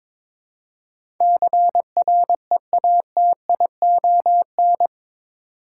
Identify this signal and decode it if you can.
Morse code, heard as a keyed tone, about 22 wpm, 700 Hz. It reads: CREATION
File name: practice / 22wpm / Eight Letter Words - 22wpm